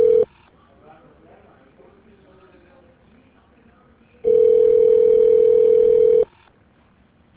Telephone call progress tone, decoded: ringback tone

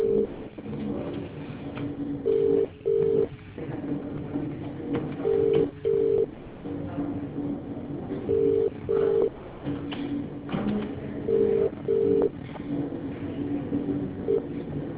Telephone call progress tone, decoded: ringback tone

